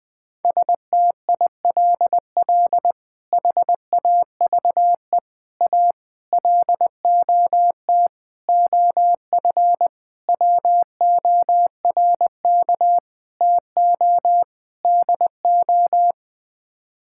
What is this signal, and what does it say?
Morse code, heard as a keyed tone, about 20 wpm, 700 Hz. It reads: STILL HAVE A LOT OF WORK TO DO
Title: Sets of 8 Words - 20wpm